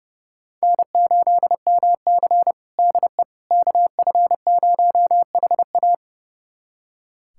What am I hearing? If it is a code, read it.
Morse code, heard as a keyed tone, about 31 wpm, 700 Hz. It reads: N8MC DE KF0HA
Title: USA Callsign DE Callsign 30wpm